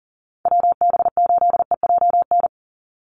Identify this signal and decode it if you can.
Morse code, heard as a keyed tone, about 40 wpm, 700 Hz. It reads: WB8EJN
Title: Call Signs - 40wpm